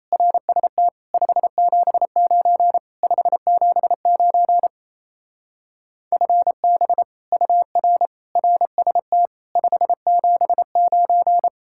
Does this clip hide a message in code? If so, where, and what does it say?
Yes: Morse code from 0.1 to 11.5 s, heard as a keyed tone, about 35 wpm, 700 Hz. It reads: RST 579 579 FB UR RST 579